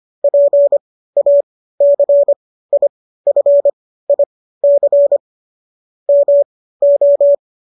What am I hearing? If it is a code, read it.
Morse code, heard as a keyed tone, about 25 wpm, spaced as for 15 wpm, 570 Hz. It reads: PACIFIC MO
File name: qso213 15